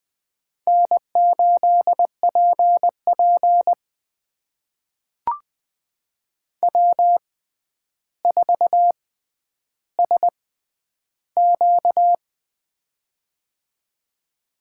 Morse code, heard as a keyed tone, about 20 wpm, spaced as for 10 wpm, 700 Hz. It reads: N8PP W4SQ
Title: USA Call Signs - 4 Characters - 6x Character Spacing - 20wpm